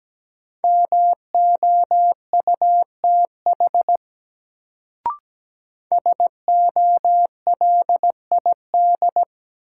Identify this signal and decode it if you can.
Morse code, heard as a keyed tone, about 17 wpm, 700 Hz. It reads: MOUTH SOLID
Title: Sentences from Top 1500 Words - New Word Drill - 17wpm